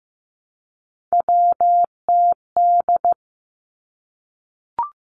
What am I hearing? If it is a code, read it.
Morse code, heard as a keyed tone, about 15 wpm, 700 Hz. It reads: WTD
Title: Three Challenging Letter-Number Combinations 15wpm